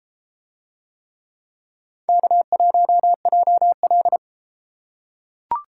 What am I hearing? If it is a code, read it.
Morse code, heard as a keyed tone, about 29 wpm, 700 Hz. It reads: K1JL